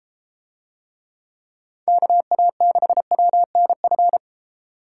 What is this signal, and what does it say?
Morse code, heard as a keyed tone, about 29 wpm, 700 Hz. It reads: KA6WNF